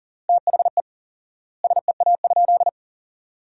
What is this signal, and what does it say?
Morse code, heard as a keyed tone, about 40 wpm, 700 Hz. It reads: THE SEA?
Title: 4x Word-Spacing - Sentences from Top 300 Words - 40wpm